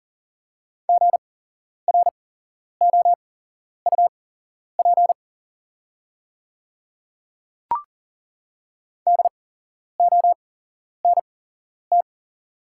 Morse code, heard as a keyed tone, about 40 wpm, spaced as for 11 wpm, 700 Hz. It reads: GROUP DONT
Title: 8x Character Spacing - Top 300 Words - 40wpm